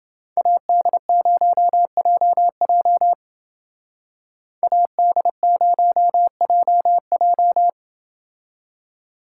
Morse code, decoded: AD0JJ AD0JJ

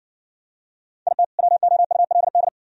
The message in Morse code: IT WOULD

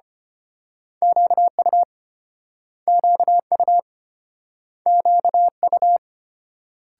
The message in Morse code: QU QU QU

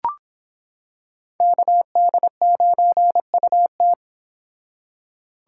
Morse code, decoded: KD9UT

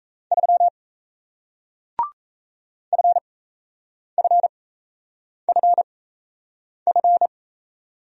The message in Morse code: 3 FFFF

